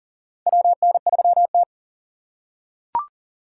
Morse code, heard as a keyed tone, about 39 wpm, 690 Hz. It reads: WN3T
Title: USA Call Signs - 4 Characters - 2x Character Spacing - 40wpm